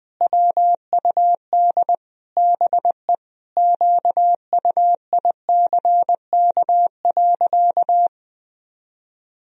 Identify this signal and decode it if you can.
Morse code, heard as a keyed tone, about 20 wpm, 700 Hz. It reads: WUD BE QUICK.
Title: Sentences from Top 500 Words and Abbreviations 20wpm